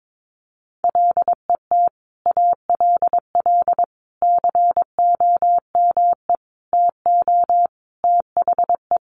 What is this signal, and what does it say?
Morse code, heard as a keyed tone, about 22 wpm, 700 Hz. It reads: LET ALL COME TO THE
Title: Sentences from Top 1100 Words 22wpm